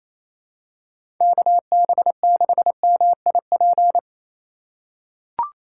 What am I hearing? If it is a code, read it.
Morse code, heard as a keyed tone, about 27 wpm, 700 Hz. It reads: KB6MIP
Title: USA Call Signs - 4x Character Spacing - 28wpm